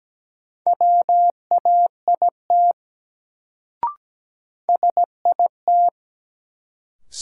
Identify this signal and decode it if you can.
Morse code, heard as a keyed tone, about 17 wpm, 700 Hz. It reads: WAIT SIT
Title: Sentences from Top 500 Words - New Word Drill - 17wpm